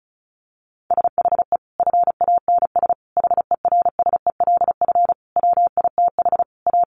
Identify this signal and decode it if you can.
Morse code, heard as a keyed tone, about 34 wpm, 700 Hz. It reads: SHE FANS HERSELF WITH A
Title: Additional Easy Phrases 35wpm